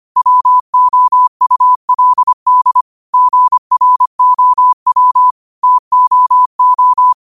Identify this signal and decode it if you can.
Morse code, heard as a keyed tone, about 25 wpm, 1000 Hz. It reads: WOULD GROW TOO
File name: qotdfiles / 25wpm